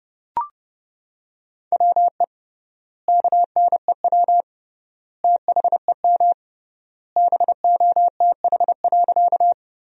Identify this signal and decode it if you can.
Morse code, heard as a keyed tone, about 32 wpm, 700 Hz. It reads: WE KNEW THEM BOTH.
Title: 3x Word-Spacing - Sentences from Top 400 Words - 30wpm